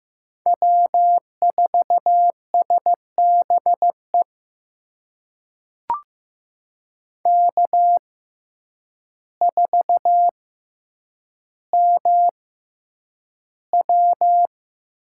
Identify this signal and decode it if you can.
Morse code, heard as a keyed tone, about 15 wpm, spaced as for 8 wpm, 700 Hz. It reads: W4SBE K4MW
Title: USA Call Signs - 6x Character Spacing - 15wpm